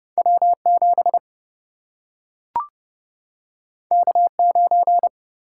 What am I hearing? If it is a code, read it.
Morse code, heard as a keyed tone, about 30 wpm, 700 Hz. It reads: W7 K9
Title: Single-Letter Single-Number 30wpm